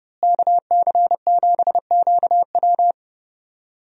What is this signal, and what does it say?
Morse code, heard as a keyed tone, about 30 wpm, 700 Hz. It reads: KC7QW
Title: USA Call Signs - 5 Characters - Repeated 3x - 30wpm